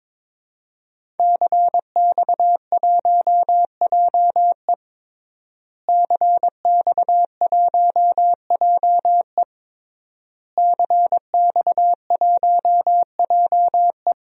Morse code, heard as a keyed tone, about 22 wpm, 700 Hz. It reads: CX1JE CX1JE CX1JE
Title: DXCC-100 Call Signs - 5 Characters - Repeated 3x - 22wpm